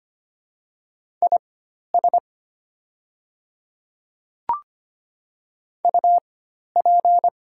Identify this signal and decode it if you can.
Morse code, heard as a keyed tone, about 25 wpm, spaced as for 12 wpm, 700 Hz. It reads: IS UP
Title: 4x Character Spacing - Top 100 Words - 25wpm